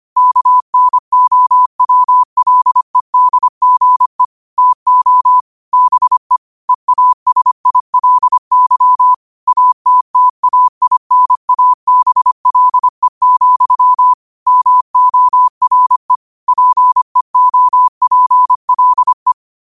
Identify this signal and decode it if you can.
Morse code, heard as a keyed tone, about 25 wpm, 1000 Hz. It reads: KNOWLEDGE TO BE EASILY ATTAINABLE, MORE PEOPLE
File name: qotdfiles / 25wpm